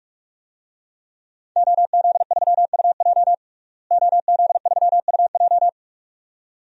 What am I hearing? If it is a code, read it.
Morse code, heard as a keyed tone, about 46 wpm, 700 Hz. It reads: OZ3UJ OZ3UJ
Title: DXCC-100 Call Signs - 5 Characters - Repeated 2x - 45wpm